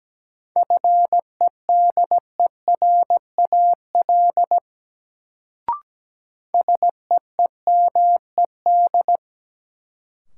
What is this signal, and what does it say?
Morse code, heard as a keyed tone, about 17 wpm, 700 Hz. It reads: FEDERAL SEEMED
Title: Sets of 3 Words - New Word Drill - 17wpm